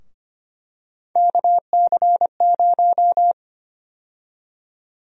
Morse code, heard as a keyed tone, about 25 wpm, 700 Hz. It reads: KC0
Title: USA Call Sign Prefixes - Repeated 2x - 25wpm